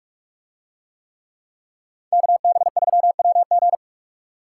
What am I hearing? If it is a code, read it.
Morse code, heard as a keyed tone, about 43 wpm, 700 Hz. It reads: KB3WG